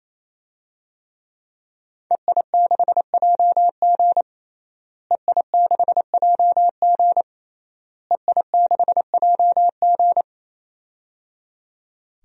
Morse code, heard as a keyed tone, about 27 wpm, 700 Hz. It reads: EI6JG EI6JG EI6JG
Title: DXCC-100 Call Signs - 5 Characters - Repeated 3x - 28wpm